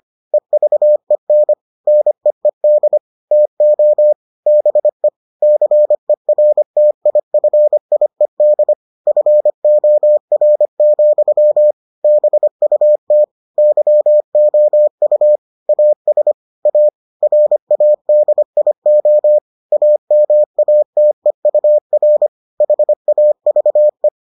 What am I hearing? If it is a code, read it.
Morse code, heard as a keyed tone, about 25 wpm, 600 Hz. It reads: EVEN NEED TO BE CERTIFIED FOR, BUT YOU AS A RADIO AMATEUR HAVE